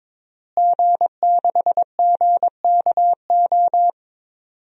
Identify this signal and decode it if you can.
Morse code, heard as a keyed tone, about 22 wpm, 700 Hz. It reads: G6GKO
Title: DXCC-100 Call Signs - 5 Characters - Repeated 3x - 22wpm